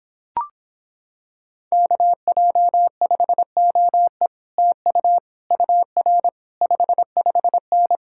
Morse code, heard as a keyed tone, about 26 wpm, 700 Hz. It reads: KJ5OE TU UR 55N